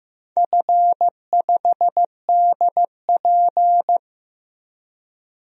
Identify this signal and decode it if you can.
Morse code, heard as a keyed tone, about 15 wpm, 700 Hz. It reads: F5DP
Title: DXCC-100 Call Signs - 4 Characters - 15wpm